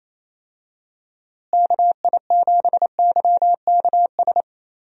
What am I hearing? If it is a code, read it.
Morse code, heard as a keyed tone, about 28 wpm, 700 Hz. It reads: KI7YKS
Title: USA Call Signs - 6 Characters - Repeated 3x - 28wpm